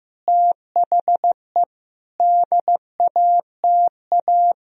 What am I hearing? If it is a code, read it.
Morse code, heard as a keyed tone, about 14 wpm, 700 Hz. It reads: THE DATA
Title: Sentences from Top 1000 Words - Review Part 3 - 15wpm